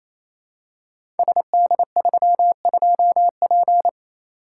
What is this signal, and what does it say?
Morse code, heard as a keyed tone, about 28 wpm, 700 Hz. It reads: SD32P
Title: Code Groups - 5 Letter:Numbers - 28wpm